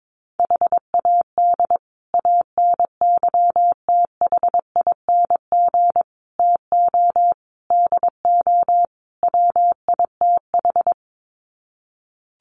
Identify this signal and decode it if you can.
Morse code, heard as a keyed tone, about 22 wpm, 700 Hz. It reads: HAD ANYTHING TO DO WITH
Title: Sets of 5 Words Encore 22wpm